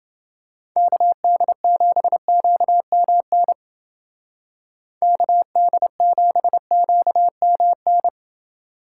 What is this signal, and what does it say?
Morse code, heard as a keyed tone, about 28 wpm, 700 Hz. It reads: KD7QMN KD7QMN